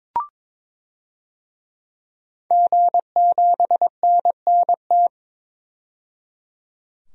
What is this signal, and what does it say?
Morse code, heard as a keyed tone, about 21 wpm, 700 Hz. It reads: G7NNT
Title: DXCC-100 Call Signs - 5 Characters - 22wpm